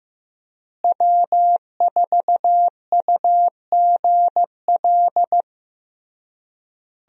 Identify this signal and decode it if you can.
Morse code, heard as a keyed tone, about 15 wpm, 700 Hz. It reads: W4UGL
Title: USA Call Signs - 5 Characters - Repeated 3x - 15wpm